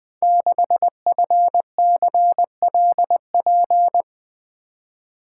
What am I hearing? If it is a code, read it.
Morse code, heard as a keyed tone, about 20 wpm, 700 Hz. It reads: 6FCLP
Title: Code Groups - 5 Letter:Numbers - 20wpm